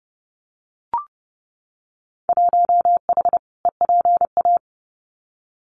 Morse code, heard as a keyed tone, about 29 wpm, 700 Hz. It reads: 1H EPA
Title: Winter Field Day Exchanges 30wpm